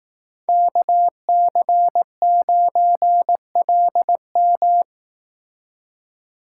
Morse code, heard as a keyed tone, about 18 wpm, 700 Hz. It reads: KC9LM
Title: Speed Racing - USA 2x2 Call Signs 15wpm